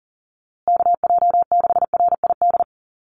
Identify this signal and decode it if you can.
Morse code, heard as a keyed tone, about 40 wpm, 700 Hz. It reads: KJ6RID
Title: USA Call Signs Repeated 3x 40wpm